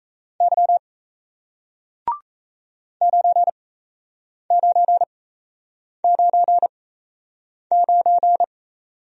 Morse code, heard as a keyed tone, about 37 wpm, spaced as for 8 wpm, 700 Hz. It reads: Y 9999